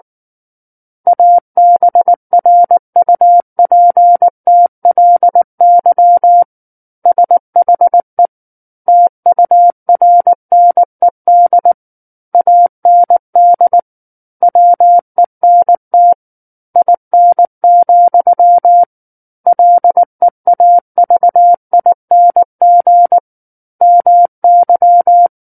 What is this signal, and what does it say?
Morse code, heard as a keyed tone, about 19 wpm, 700 Hz. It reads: ABRUPTLY SHE TURNED AND WENT IN, LEAVING MY
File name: Book One / 10-Chapter 10 - 19 WPM